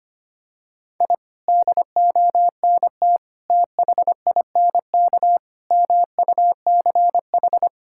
Morse code, heard as a keyed tone, about 25 wpm, 700 Hz. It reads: I DONT THINK MUCH
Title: Sentences from Top 1000 Words - Review Part 6 - 25wpm